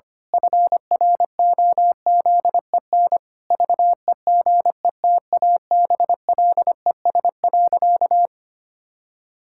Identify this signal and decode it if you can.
Morse code, heard as a keyed tone, about 25 wpm, 700 Hz. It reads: FROZEN VEGETABLES.
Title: Pangram Challenge v2 - 25wpm